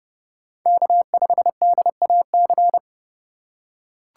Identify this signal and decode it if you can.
Morse code, heard as a keyed tone, about 30 wpm, 700 Hz. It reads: K5DAC
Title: USA Call Signs - 5 Characters - 2x Character Spacing - 30wpm